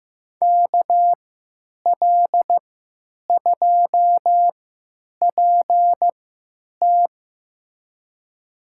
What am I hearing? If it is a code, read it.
Morse code, heard as a keyed tone, about 15 wpm, spaced as for 9 wpm, 700 Hz. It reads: KL2PT
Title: USA Call Signs - 5 Characters - 3x Character Spacing - 15wpm